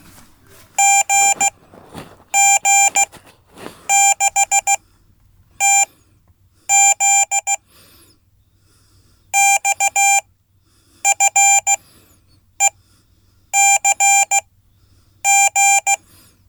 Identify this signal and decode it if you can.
Morse code, heard as a keyed tone, about 15 wpm, spaced as for 8 wpm, 790 Hz. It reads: GG6TZ XFECG
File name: Code / 5min-b